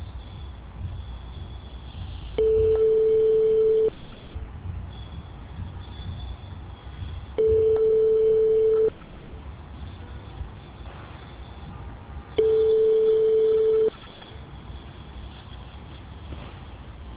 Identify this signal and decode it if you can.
Telephone call progress tone (France), heard as a telephone network's tone, about 440 Hz. Ringback tone.